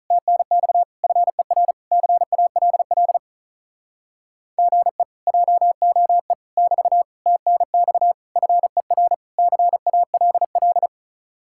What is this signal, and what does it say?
Morse code, heard as a keyed tone, about 45 wpm, 700 Hz. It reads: TNX FER CALL GE JOE <BT> TNX FER CALL